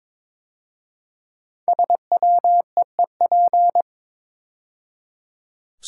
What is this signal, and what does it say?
Morse code, heard as a keyed tone, about 22 wpm, 700 Hz. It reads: SWEEP
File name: 5 Letter Words - First 3 Letters Given - 22wpm